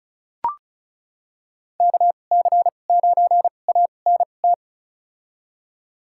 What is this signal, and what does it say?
Morse code, heard as a keyed tone, about 36 wpm, spaced as for 25 wpm, 700 Hz. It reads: KC9ANT
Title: USA Call Signs - 6 Characters - 2x Character Spacing - 35wpm